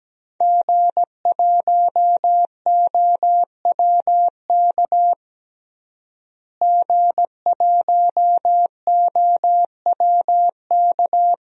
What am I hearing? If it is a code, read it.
Morse code, heard as a keyed tone, about 17 wpm, 700 Hz. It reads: G1OWK G1OWK
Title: DXCC-100 Call Signs - 5 Characters - Repeated 3x - 17wpm